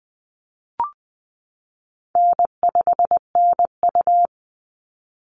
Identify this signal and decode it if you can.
Morse code, heard as a keyed tone, about 20 wpm, 700 Hz. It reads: N5NU
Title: Call Signs Repeated 3x - 20wpm